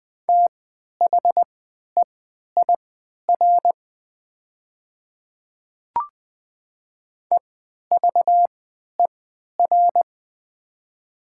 Morse code, heard as a keyed tone, about 20 wpm, spaced as for 11 wpm, 690 Hz. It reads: THEIR EVER
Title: 3x Character Spacing - Top 500 Words - 20wpm